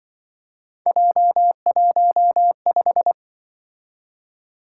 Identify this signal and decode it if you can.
Morse code, heard as a keyed tone, about 26 wpm, 700 Hz. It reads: J15